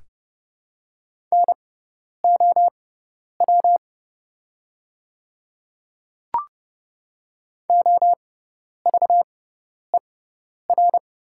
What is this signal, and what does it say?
Morse code, heard as a keyed tone, about 30 wpm, spaced as for 10 wpm, 690 Hz. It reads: NOW OVER
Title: 6x Character Spacing - Top 100 Words - 30wpm